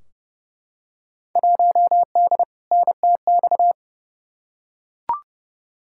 Morse code, heard as a keyed tone, about 30 wpm, 700 Hz. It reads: 1D NTX
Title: Field Day Exchanges - 30wpm